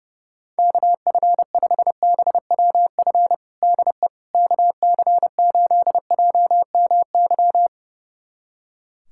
Morse code, heard as a keyed tone, about 30 wpm, 700 Hz. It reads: KF5BWF DE KC8JMY
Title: USA Callsign DE Callsign - 6 Characters - 30wpm